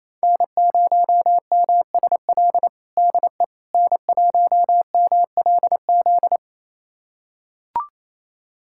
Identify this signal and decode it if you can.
Morse code, heard as a keyed tone, about 28 wpm, 700 Hz. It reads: N0MSL DE N1MLZ
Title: USA Callsign DE Callsign - 5 Characters - 28wpm